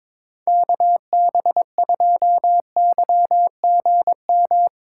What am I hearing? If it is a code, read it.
Morse code, heard as a keyed tone, about 22 wpm, 700 Hz. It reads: KB2YGM
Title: USA Call Signs - 6 Characters - 4x Character Spacing - 22wpm